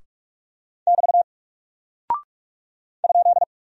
Morse code, heard as a keyed tone, about 48 wpm, 700 Hz. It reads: <BT> ?
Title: Single Character (Letter:Number:Punctuation:Prosign) Alt 45wpm